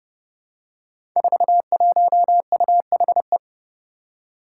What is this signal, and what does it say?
Morse code, heard as a keyed tone, about 29 wpm, 700 Hz. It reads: 41UHE
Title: Five Challenging Letter-Number Combinations 30wpm